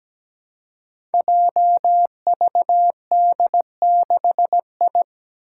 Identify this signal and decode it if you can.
Morse code, heard as a keyed tone, about 17 wpm, 700 Hz. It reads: JVD6I
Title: Five Challenging Letter-Number Combinations 17wpm